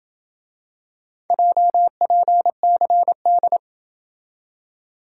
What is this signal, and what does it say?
Morse code, heard as a keyed tone, about 26 wpm, 700 Hz. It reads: JPCD